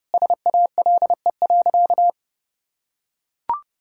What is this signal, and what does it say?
Morse code, heard as a keyed tone, about 30 wpm, 700 Hz. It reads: SALE.
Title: Sentences from Top 1600 Words 30wpm